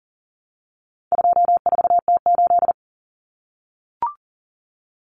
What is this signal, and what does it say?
Morse code, heard as a keyed tone, about 40 wpm, 700 Hz. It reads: 24T8
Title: Common Cut Numbers - 4-Digit 40wpm